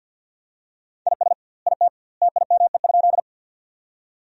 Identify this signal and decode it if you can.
Morse code, heard as a keyed tone, about 50 wpm, 700 Hz. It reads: IS IT TIME?
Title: 2x Word-Spacing - Sentences from Top 100 Words - 50wpm